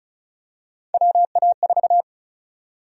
Morse code, heard as a keyed tone, about 36 wpm, 700 Hz. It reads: WA4
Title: USA Call Sign Prefixes - Repeated 2x - 35wpm